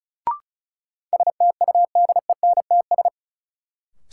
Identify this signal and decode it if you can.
Morse code, heard as a keyed tone, about 36 wpm, 700 Hz. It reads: STUDENTS
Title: Sets of 5 Words Encore - New Word Drill - 35wpm